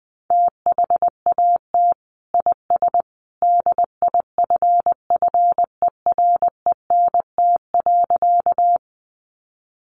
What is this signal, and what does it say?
Morse code, heard as a keyed tone, about 20 wpm, 700 Hz. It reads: THAT IS DIFFERENT.
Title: Sentences from Top 600 Words 20wpm